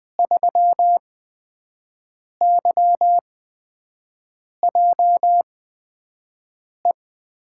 Morse code, heard as a keyed tone, about 20 wpm, spaced as for 5 wpm, 700 Hz. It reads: 3YJE
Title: USA Call Signs - 5 Characters - 8x Character Spacing - 20wpm